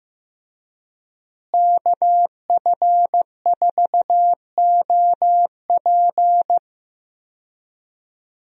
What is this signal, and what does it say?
Morse code, heard as a keyed tone, about 15 wpm, 700 Hz. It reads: KF4OP
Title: USA Call Signs - 5 Characters - Repeated 3x - 15wpm